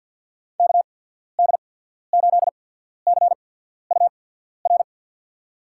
Morse code, heard as a keyed tone, about 53 wpm, spaced as for 14 wpm, 700 Hz. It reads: KD8CUR
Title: USA Call Signs - 6 Characters - 8x Character Spacing - 50wpm